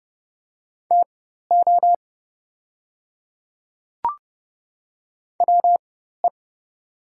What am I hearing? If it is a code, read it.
Morse code, heard as a keyed tone, about 30 wpm, spaced as for 14 wpm, 700 Hz. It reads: TO WE
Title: 4x Character Spacing - Top 300 Words - 30wpm